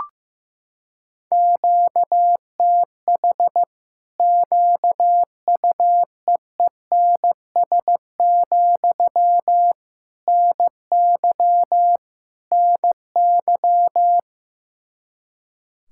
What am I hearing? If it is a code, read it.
Morse code, heard as a keyed tone, about 15 wpm, 700 Hz. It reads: QTH QUEENS, NY NY
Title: Practice QSO - Exchange 1 - Beginning - 15wpm